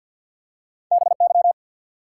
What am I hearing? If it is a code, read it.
Morse code, heard as a keyed tone, about 48 wpm, 700 Hz. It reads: BY
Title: Top 180 QSO Elements 50wpm